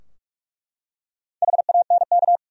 Morse code, heard as a keyed tone, about 41 wpm, 700 Hz. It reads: HANK
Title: USA Common Names - Max 5 Letters - 45wpm